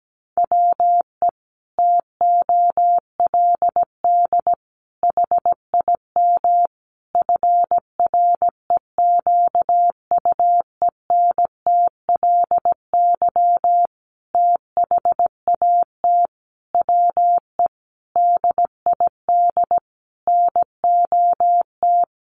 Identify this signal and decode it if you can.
Morse code, heard as a keyed tone, about 17 wpm, 700 Hz. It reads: WE TOLD HIM FREQUENTLY THAT WE DID NOT
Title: Sentences from Top 1300 Words 17wpm